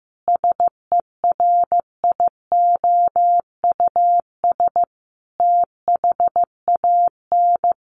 Morse code, heard as a keyed tone, about 15 wpm, 700 Hz. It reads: SERIOUS THAN
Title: Sentences from Top 1100 Words 15wpm